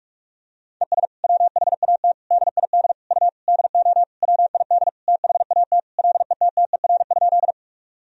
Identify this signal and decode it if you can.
Morse code, heard as a keyed tone, about 46 wpm, 700 Hz. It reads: ES WHAT DID U DO WID THAT LETTER?
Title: Sentences from Top 300 Words and Abbreviations 45wpm